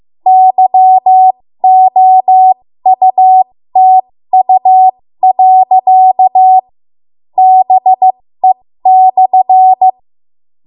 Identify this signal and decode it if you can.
Morse code, heard as a keyed tone, about 15 wpm, spaced as for 12 wpm, 750 Hz. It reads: YOUTU.BE/
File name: Morse / 13WPM